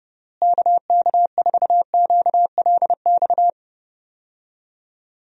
Morse code, heard as a keyed tone, about 32 wpm, 700 Hz. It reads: KK4QLX
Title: USA Call Signs - 6 Characters - Repeated 3x - 30wpm